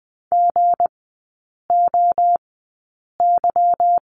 Morse code, heard as a keyed tone, about 20 wpm, spaced as for 8 wpm, 700 Hz. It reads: GOY
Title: 178 Lesson 29e - Slash - Three Characters 20wpm